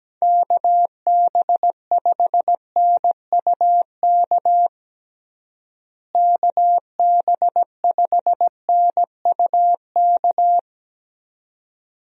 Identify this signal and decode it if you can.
Morse code, heard as a keyed tone, about 17 wpm, 700 Hz. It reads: KB5NUK KB5NUK
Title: USA Call Signs - 6 Characters - Repeated 2x - 17wpm